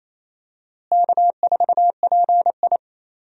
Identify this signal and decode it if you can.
Morse code, heard as a keyed tone, about 27 wpm, 700 Hz. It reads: K4PI
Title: USA Call Signs - 4 Characters - Repeated 2x - 28wpm